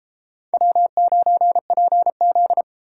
Morse code, heard as a keyed tone, about 29 wpm, 700 Hz. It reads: W9PZ